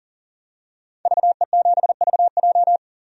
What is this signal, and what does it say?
Morse code, heard as a keyed tone, about 39 wpm, 700 Hz. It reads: VE7VJ
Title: DXCC-100 Call Signs - 5 Characters - 40wpm